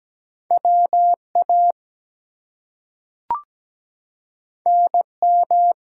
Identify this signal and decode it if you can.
Morse code, heard as a keyed tone, about 17 wpm, 700 Hz. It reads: WA NM
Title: US States and Canadian Provinces 17wpm